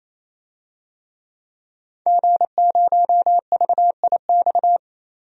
Morse code, heard as a keyed tone, about 28 wpm, 700 Hz. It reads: G0VIX
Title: DXCC-100 Call Signs - 5 Characters - Repeated 3x - 28wpm